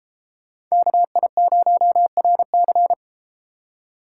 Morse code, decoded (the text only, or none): KI0RC